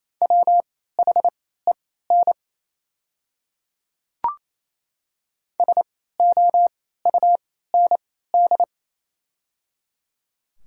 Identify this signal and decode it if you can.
Morse code, heard as a keyed tone, about 28 wpm, spaced as for 16 wpm, 700 Hz. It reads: WHEN SOUND
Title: 3x Character Spacing - Top 200 Words - 28wpm